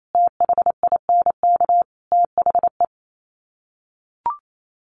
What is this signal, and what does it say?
Morse code, heard as a keyed tone, about 28 wpm, 700 Hz. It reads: THINK THE
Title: Sets of 5 Words Encore 28wpm